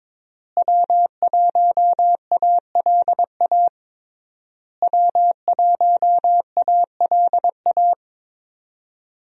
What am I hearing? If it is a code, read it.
Morse code, heard as a keyed tone, about 22 wpm, 700 Hz. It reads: W1ALA W1ALA
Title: USA Call Signs - 5 Characters - Repeated 3x - 22wpm